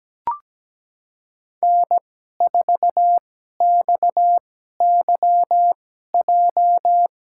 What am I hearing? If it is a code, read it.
Morse code, heard as a keyed tone, about 17 wpm, spaced as for 12 wpm, 700 Hz. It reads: N4XYJ
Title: USA Call Signs - 5 Characters - 2x Character Spacing - 17wpm